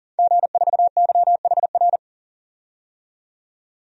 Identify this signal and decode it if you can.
Morse code, heard as a keyed tone, about 40 wpm, 700 Hz. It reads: G4YHR
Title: DXCC-100 Call Signs - 5 Characters - 40wpm